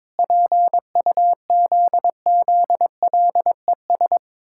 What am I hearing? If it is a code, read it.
Morse code, heard as a keyed tone, about 22 wpm, 700 Hz. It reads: PUZZLES
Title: Pangram Challenge v2 - 22wpm